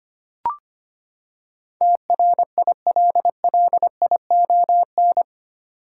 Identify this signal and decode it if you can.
Morse code, heard as a keyed tone, about 25 wpm, 700 Hz. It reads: TRILLION
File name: Eight Letter Words - 25wpm